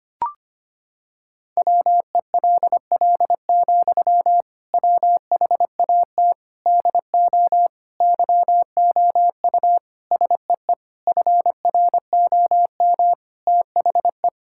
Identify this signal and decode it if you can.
Morse code, heard as a keyed tone, about 25 wpm, 700 Hz. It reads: WELL, WHAT DO YOU SEE FROM THE
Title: Sentences from Top 1600 Words 25wpm